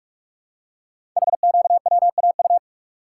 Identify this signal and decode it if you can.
Morse code, heard as a keyed tone, about 45 wpm, 700 Hz. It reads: HQWAU
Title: Code Groups - 5 Letters - 45wpm